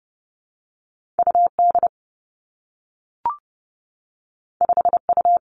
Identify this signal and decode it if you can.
Morse code, heard as a keyed tone, about 30 wpm, 700 Hz. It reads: U D 5 U
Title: Two Challenging Letter-Number Combinations 30wpm